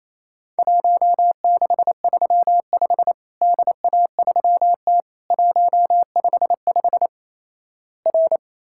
Morse code, heard as a keyed tone, about 28 wpm, 700 Hz. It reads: 1635 DA3T 155 R
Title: Worked All Europe DX Contest - QTC Exchange - 28wpm